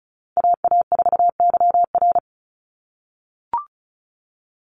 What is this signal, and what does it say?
Morse code, heard as a keyed tone, about 36 wpm, 700 Hz. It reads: AA4YR
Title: USA Call Signs 35wpm